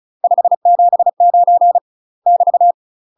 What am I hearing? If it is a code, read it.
Morse code, heard as a keyed tone, about 34 wpm, 700 Hz. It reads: 579 <BT>
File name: UK MASTER QSOs TRAINING / 35WPM / UK MASTER QSOs 0009